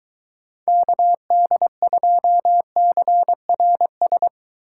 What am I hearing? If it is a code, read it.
Morse code, heard as a keyed tone, about 23 wpm, 700 Hz. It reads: KD2CRS